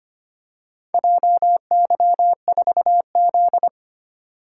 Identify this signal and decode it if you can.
Morse code, heard as a keyed tone, about 25 wpm, 700 Hz. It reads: JY4Z
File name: practice / 25wpm / Four Letter-Number Combinations - 25wpm